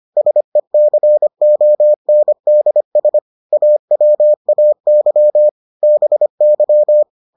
Morse code, heard as a keyed tone, about 25 wpm, 600 Hz. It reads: SECONDS AWAY BY